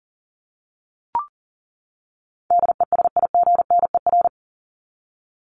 Morse code, heard as a keyed tone, about 40 wpm, 700 Hz. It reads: DESIGNER
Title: Top 1501-1750 Words 40wpm